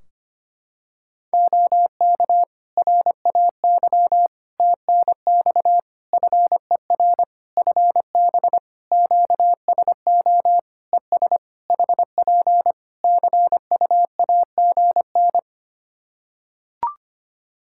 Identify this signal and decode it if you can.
Morse code, heard as a keyed tone, about 25 wpm, 700 Hz. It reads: OK RAY TNX FER FB QSO ES HP CUAGN
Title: Practice QSO - Exchange 4 - Ending - 25wpm